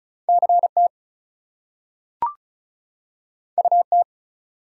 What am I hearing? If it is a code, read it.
Morse code, heard as a keyed tone, about 36 wpm, 700 Hz. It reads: CT UT
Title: US States and Canadian Provinces 35wpm